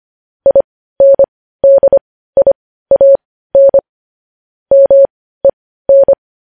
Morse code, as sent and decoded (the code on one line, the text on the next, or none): .. -. -.. .. .- -. / -- . -.
INDIAN MEN